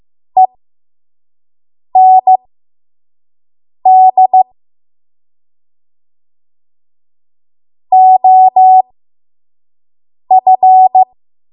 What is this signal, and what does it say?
Morse code, heard as a keyed tone, about 15 wpm, spaced as for 5 wpm, 750 Hz. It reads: END OF
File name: Morse / 05WPM